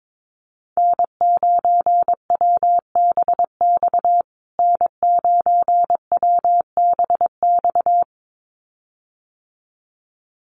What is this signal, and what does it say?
Morse code, heard as a keyed tone, about 22 wpm, 700 Hz. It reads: N9WBX N9WBX
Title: USA Call Signs Repeated 2x 22wpm